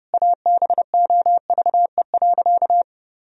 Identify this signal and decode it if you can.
Morse code, heard as a keyed tone, about 30 wpm, 700 Hz. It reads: ABOVE.
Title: Sentences from Top 1000 Words - Review Part 2 - 30wpm